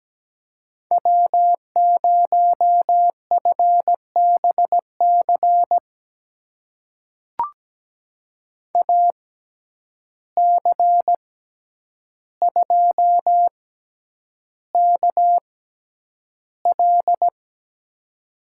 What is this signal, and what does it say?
Morse code, heard as a keyed tone, about 17 wpm, spaced as for 9 wpm, 700 Hz. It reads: W0FBC AC2KL
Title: USA Call Signs - 5 Characters - 6x Character Spacing - 17wpm